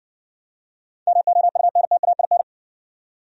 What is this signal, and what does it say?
Morse code, heard as a keyed tone, about 60 wpm, 700 Hz. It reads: MOUNTAIN